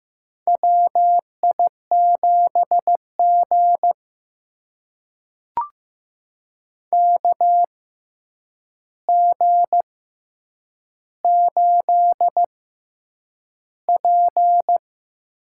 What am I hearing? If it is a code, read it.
Morse code, heard as a keyed tone, about 15 wpm, spaced as for 8 wpm, 700 Hz. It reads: WI7G KG8P
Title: USA Call Signs - 6x Character Spacing - 15wpm